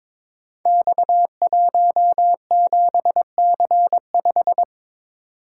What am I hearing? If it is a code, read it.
Morse code, heard as a keyed tone, about 22 wpm, 700 Hz. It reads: X17C5
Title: Code Groups - 5 Letter:Numbers - 22wpm